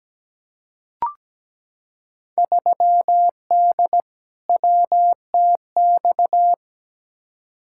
Morse code, heard as a keyed tone, about 17 wpm, 700 Hz. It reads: 3D WTX
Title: Field Day Exchanges Encore - 17wpm